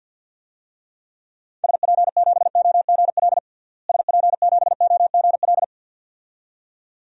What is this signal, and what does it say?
Morse code, heard as a keyed tone, about 47 wpm, 700 Hz. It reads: SP7OGL SP7OGL
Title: DXCC-100 Call Signs - 6 Characters - Repeated 2x - 50wpm